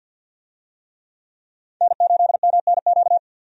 Morse code, heard as a keyed tone, about 47 wpm, 700 Hz. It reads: N8MNQ
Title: USA Call Signs - 5 Characters - 50wpm